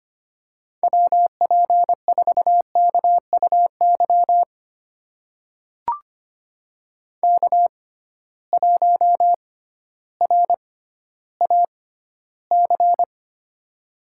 Morse code, heard as a keyed tone, about 25 wpm, spaced as for 13 wpm, 700 Hz. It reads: WP4KUY K1RAC
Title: USA Call Signs - 6x Character Spacing - 25wpm